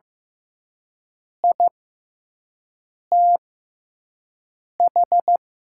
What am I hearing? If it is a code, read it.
Morse code, heard as a keyed tone, about 15 wpm, spaced as for 5 wpm, 700 Hz. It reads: ITH
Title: 6x Character Spacing - Top 200 Words - 15wpm